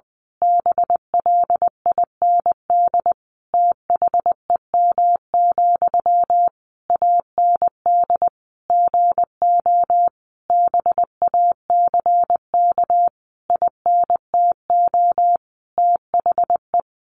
Morse code, heard as a keyed tone, about 20 wpm, 700 Hz. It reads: BLIND THEM, AND GO BACK INTO THE
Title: A Sound Of Thunder 20wpm